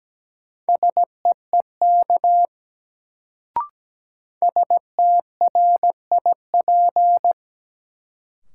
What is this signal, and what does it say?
Morse code, heard as a keyed tone, about 17 wpm, 700 Hz. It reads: SEEK STRIP
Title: Sentences from Top 1600 Words - New Word Drill - 17wpm